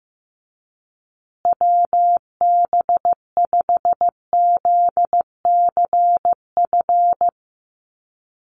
Morse code, heard as a keyed tone, about 15 wpm, 700 Hz. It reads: WB5ZCF
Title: USA Call Signs Repeated 3x 15wpm